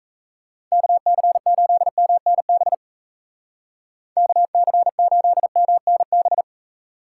Morse code, heard as a keyed tone, about 37 wpm, 700 Hz. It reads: KC8MNB KC8MNB